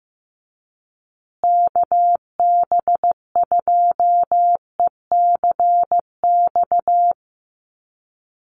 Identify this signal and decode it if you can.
Morse code, heard as a keyed tone, about 15 wpm, 700 Hz. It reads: KB2ECX
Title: USA Call Signs Repeated 3x 15wpm